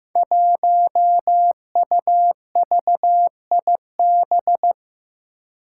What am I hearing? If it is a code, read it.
Morse code, heard as a keyed tone, about 15 wpm, 700 Hz. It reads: 1UVIB
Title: Five Challenging Letter-Number Combinations 15wpm